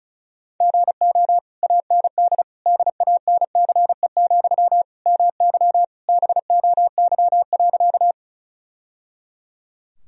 Morse code, heard as a keyed tone, about 34 wpm, 700 Hz. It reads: GO AND DANCE, MY BOY.
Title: Sentences from Top 1000 Words - Review Part 4 - 35wpm